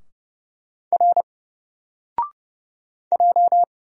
Morse code, heard as a keyed tone, about 30 wpm, 700 Hz. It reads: R J
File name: Single-Letter-Number-30wpm